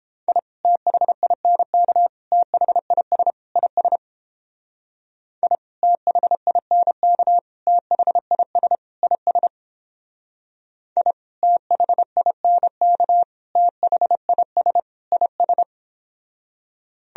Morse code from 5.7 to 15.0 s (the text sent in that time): THINK THIS IS I THINK THIS